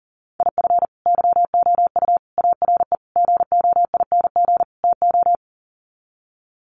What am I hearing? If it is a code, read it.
Morse code, heard as a keyed tone, about 40 wpm, 700 Hz. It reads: IF YOU ARE GOING TO
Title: Sets of 5 Words Encore2 40wpm